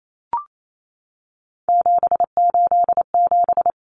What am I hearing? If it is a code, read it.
Morse code, heard as a keyed tone, about 28 wpm, 700 Hz. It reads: 787
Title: Common Cut Numbers - 3-Digit 28wpm